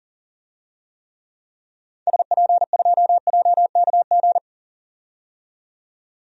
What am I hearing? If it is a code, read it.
Morse code, heard as a keyed tone, about 38 wpm, 700 Hz. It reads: SP2JKG
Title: DXCC-100 Call Signs - 6 Characters - 40wpm